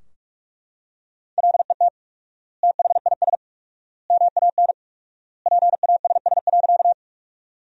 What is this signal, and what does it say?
Morse code, heard as a keyed tone, about 42 wpm, 700 Hz. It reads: LET THIS MAN PASS.
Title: 4x Word-Spacing - Sentences from Top 400 Words - 45wpm